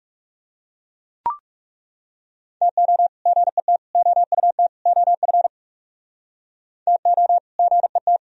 Morse code, heard as a keyed tone, about 39 wpm, 700 Hz. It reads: TO GET OUT OF TO GET